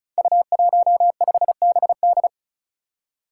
Morse code, heard as a keyed tone, about 36 wpm, 700 Hz. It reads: U15BD
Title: Five Challenging Letter-Number Combinations 35wpm